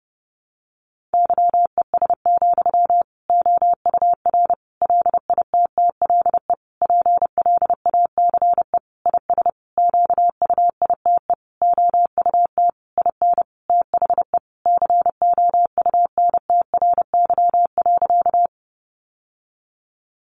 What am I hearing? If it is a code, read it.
Morse code, heard as a keyed tone, about 28 wpm, 700 Hz. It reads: YES, OUR LITTLE PLACE IS QUITE OUT IN THE COUNTRY.
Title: Sentences from Top 700 Words 30wpm